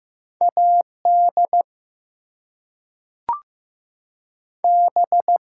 Morse code, heard as a keyed tone, about 15 wpm, 700 Hz. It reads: AD B
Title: Two Letter Words - 15wpm